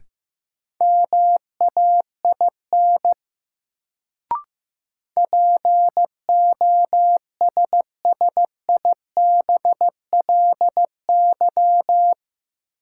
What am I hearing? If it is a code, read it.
Morse code, heard as a keyed tone, about 15 wpm, 700 Hz. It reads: MAIN POSSIBLY
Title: Sentences from Top 1300 Words - New Word Drill Alt - 15wpm